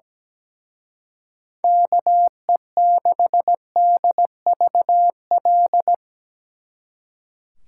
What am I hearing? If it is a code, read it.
Morse code, heard as a keyed tone, about 19 wpm, 700 Hz. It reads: KE6DVL
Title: Speed Racing - USA 2x3 Call Signs 17wpm